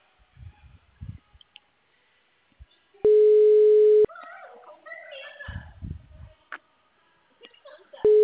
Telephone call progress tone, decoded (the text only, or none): ringback tone